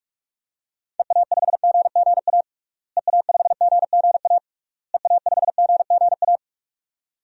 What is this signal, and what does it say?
Morse code, heard as a keyed tone, about 46 wpm, 700 Hz. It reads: EA5GGA EA5GGA EA5GGA
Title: DXCC-100 Call Signs - 6 Characters - Repeated 3x - 45wpm